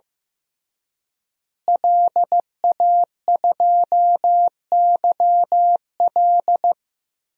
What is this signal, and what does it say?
Morse code, heard as a keyed tone, about 15 wpm, 700 Hz. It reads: LA2YL
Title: DXCC-100 Call Signs - 5 Characters - Repeated 2x - 15wpm